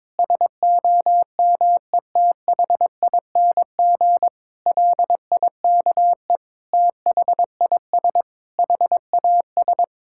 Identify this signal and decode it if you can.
Morse code, heard as a keyed tone, about 22 wpm, 700 Hz. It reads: SOMETHING LIKE THIS HAS
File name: practice / 22wpm / Sets of 8 Words - 22wpm